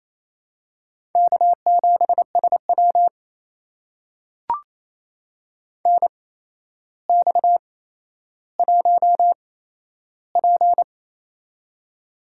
Morse code, heard as a keyed tone, about 27 wpm, spaced as for 12 wpm, 700 Hz. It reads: K7SW NX1P
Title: USA Call Signs - 4 Characters - 8x Character Spacing - 28wpm